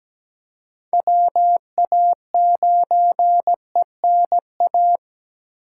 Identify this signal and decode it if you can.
Morse code, heard as a keyed tone, about 17 wpm, 700 Hz. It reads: WA9ENA
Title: K1USN SST QSO Practice - 17wpm